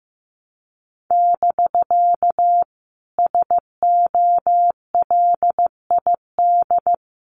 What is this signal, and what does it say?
Morse code, heard as a keyed tone, about 15 wpm, 700 Hz. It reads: <BK> SOLID
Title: Common QSO Phrases Encore 15wpm